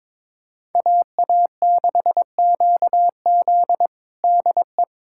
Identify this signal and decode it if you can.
Morse code, heard as a keyed tone, about 21 wpm, 700 Hz. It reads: AA6QZ DE
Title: USA Callsign DE Callsign - 5 Characters - 22wpm